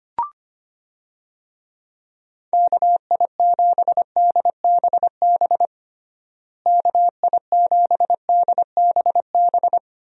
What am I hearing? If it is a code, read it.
Morse code, heard as a keyed tone, about 25 wpm, 700 Hz. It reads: KI7DBB KI7DBB
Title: USA Call Signs - 6 Characters - Repeated 2x - 25wpm